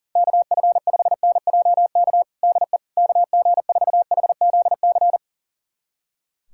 Morse code, heard as a keyed tone, about 37 wpm, 700 Hz. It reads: KF5NJK DE KG4HZC
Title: USA Callsign DE Callsign - 6 Characters - 40wpm